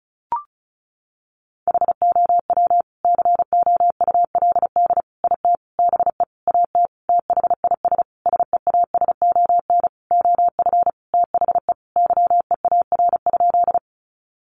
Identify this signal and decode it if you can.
Morse code, heard as a keyed tone, about 34 wpm, 700 Hz. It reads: HOW COULD IT BE AT THIS SEASON OF THE YEAR?
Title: Sentences from Top 1400 Words 35wpm